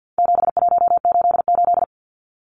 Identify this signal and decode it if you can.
Morse code, heard as a keyed tone, about 50 wpm, 700 Hz. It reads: 7188
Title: Numbers - 4-Digit 50wpm